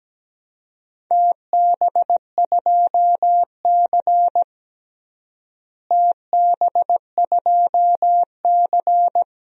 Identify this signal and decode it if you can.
Morse code, heard as a keyed tone, about 17 wpm, 700 Hz. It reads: TB2C TB2C
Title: DXCC-100 Call Signs - 4 Characters - Repeated 2x - 17wpm